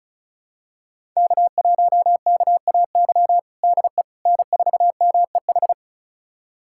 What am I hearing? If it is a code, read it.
Morse code, heard as a keyed tone, about 36 wpm, 700 Hz. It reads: K1KAY DE N4MEH
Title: USA Callsign DE Callsign - 5 Characters - 35wpm